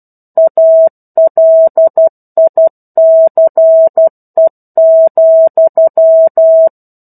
Morse code, heard as a keyed tone, about 12 wpm, 650 Hz. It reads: ALICE,
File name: 01-Chapter-01 12 WPM